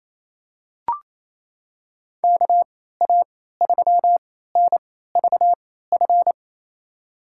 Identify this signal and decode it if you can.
Morse code, heard as a keyed tone, about 28 wpm, spaced as for 16 wpm, 700 Hz. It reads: KA3NVF
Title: USA Call Signs - 6 Characters - 3x Character Spacing - 28wpm